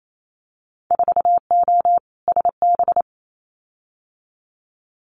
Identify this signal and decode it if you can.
Morse code, heard as a keyed tone, about 28 wpm, 700 Hz. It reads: 4O SB
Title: Winter Field Day Exchanges 28wpm